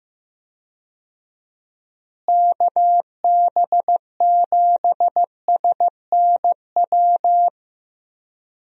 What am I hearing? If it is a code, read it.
Morse code, heard as a keyed tone, about 15 wpm, 700 Hz. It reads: KB7SNW
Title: USA Call Signs - 6 Characters - Repeated 3x - 15wpm